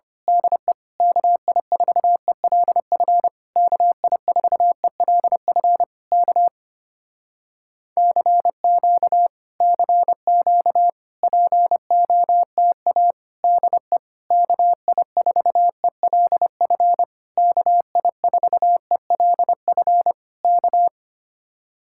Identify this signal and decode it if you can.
Morse code, heard as a keyed tone, about 26 wpm, 700 Hz. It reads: DE KI4ELF KI4ELF K CQ CQ POTA DE KI4ELF KI4ELF K
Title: Speed-Racing Common QSO Phrases - 25wpm